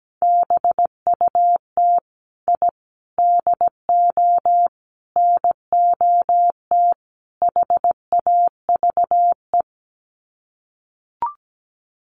Morse code, heard as a keyed tone, about 17 wpm, 700 Hz. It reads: BUT I DO NOT HAVE
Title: Sets of 5 Words Encore 17wpm